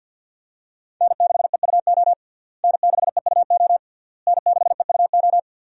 Morse code, heard as a keyed tone, about 51 wpm, 700 Hz. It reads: N6EUO N6EUO N6EUO
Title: USA Call Signs - 5 Characters - Repeated 3x - 50wpm